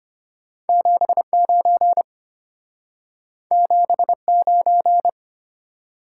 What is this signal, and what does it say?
Morse code, heard as a keyed tone, about 33 wpm, 700 Hz. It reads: 79 79